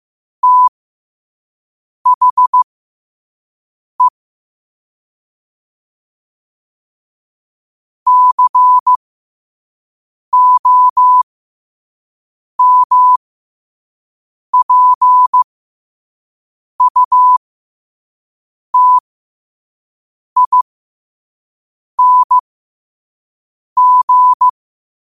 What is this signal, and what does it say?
Morse code, heard as a keyed tone, about 15 wpm, spaced as for 5 wpm, 1000 Hz. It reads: THE COMPUTING
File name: qotdfiles / 5wpm